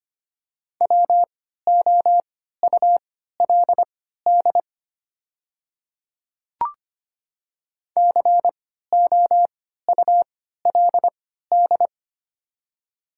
Morse code, heard as a keyed tone, about 25 wpm, spaced as for 14 wpm, 700 Hz. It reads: WOULD COULD
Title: 3x Character Spacing - Top 300 Words - 25wpm